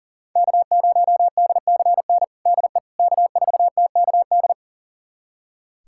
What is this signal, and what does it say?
Morse code, heard as a keyed tone, about 40 wpm, 700 Hz. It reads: K0DCN DE K4TKD
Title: USA Callsign DE Callsign - 5 Characters - 40wpm